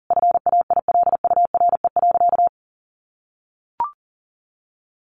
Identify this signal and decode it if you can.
Morse code, heard as a keyed tone, about 40 wpm, 700 Hz. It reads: FAILURE.
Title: Sentences from Top 1300 Words 40wpm